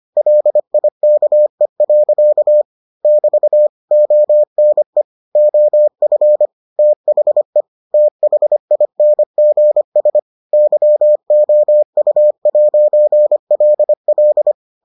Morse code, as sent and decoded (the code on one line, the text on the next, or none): .-.. .. -.- . .-.-.- / -...- / --- -. / --- ..-. / - .... / - .... .. -. --. ... / -.-- --- ..- .----. .-.. .-..
LIKE. <BT> ONE OF THE THINGS YOU'LL